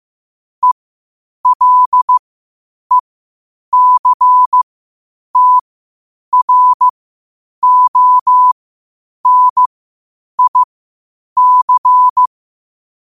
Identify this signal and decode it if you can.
Morse code, heard as a keyed tone, about 15 wpm, spaced as for 8 wpm, 1000 Hz. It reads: ELECTRONIC
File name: qotdfiles / 8wpm